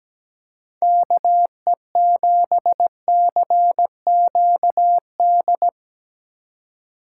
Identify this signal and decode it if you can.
Morse code, heard as a keyed tone, about 17 wpm, 700 Hz. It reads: KE7CQD
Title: USA Call Signs - 6 Characters - 3x Character Spacing - 17wpm